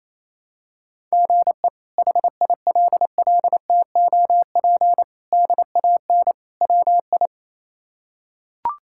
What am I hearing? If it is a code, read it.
Morse code, heard as a keyed tone, about 27 wpm, 700 Hz. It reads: GE HILLTOP DAN WI
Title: K1USN SST QSO Practice - 28wpm